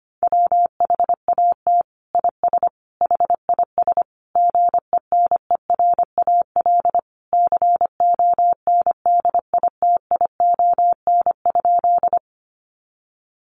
Morse code, heard as a keyed tone, about 25 wpm, 700 Hz. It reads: WHAT IS HIS GENERAL CONDITION?
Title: Sentences from Top 1000 Words 25wpm